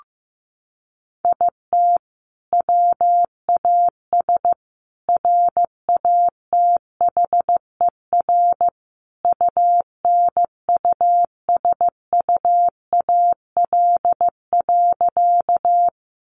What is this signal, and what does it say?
Morse code, heard as a keyed tone, about 15 wpm, 700 Hz. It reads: IT WAS RATHER UNUSUAL.
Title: Sentences from Top 1400 Words 15wpm